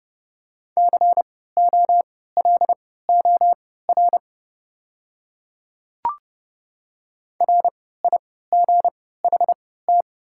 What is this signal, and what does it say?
Morse code, heard as a keyed tone, about 32 wpm, spaced as for 17 wpm, 700 Hz. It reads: COLOR RIGHT
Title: 3x Character Spacing - Top 300 Words - 30wpm